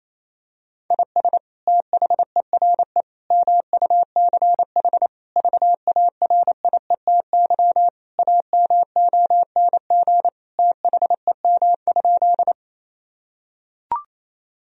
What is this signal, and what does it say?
Morse code, heard as a keyed tone, about 28 wpm, 700 Hz. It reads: IS THERE MUCH VARIETY AMONG THEM?
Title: Sentences from Top 1000 Words - Review Part 4 - 28wpm